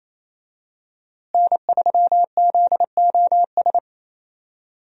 Morse code, heard as a keyed tone, about 28 wpm, 700 Hz. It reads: N3ZOS
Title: USA Call Signs - 5 Characters - Repeated 2x - 28wpm